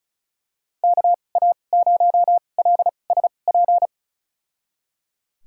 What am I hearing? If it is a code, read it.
Morse code, heard as a keyed tone, about 34 wpm, spaced as for 25 wpm, 700 Hz. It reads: KA0LSP
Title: USA Call Signs - 6 Characters - 2x Character Spacing - 35wpm